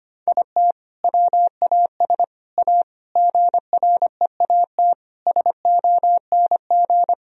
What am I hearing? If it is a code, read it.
Morse code, heard as a keyed tone, about 25 wpm, 700 Hz. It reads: IT WAS A GREAT SONG
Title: Sentences from Top 1000 Words - Review Part 4 - 25wpm